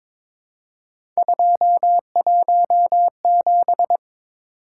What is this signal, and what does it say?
Morse code, heard as a keyed tone, about 22 wpm, 700 Hz. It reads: 217
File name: practice / 22wpm / Three Letter-Number Combinations - 22wpm